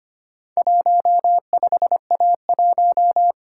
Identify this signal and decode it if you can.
Morse code, heard as a keyed tone, about 25 wpm, 700 Hz. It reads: 15A1
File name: practice / 25wpm / Four Letter-Number Combinations - 25wpm